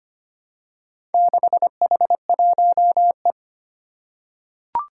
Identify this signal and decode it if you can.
Morse code, heard as a keyed tone, about 25 wpm, 700 Hz. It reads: 6H1E
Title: Four Challenging Letter-Number Combinations 25wpm